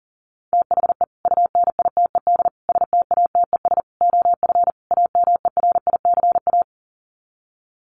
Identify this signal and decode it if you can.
Morse code, heard as a keyed tone, about 40 wpm, 700 Hz. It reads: THE UNITED STATES OF AMERICA
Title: Sets of 5 Words 40wpm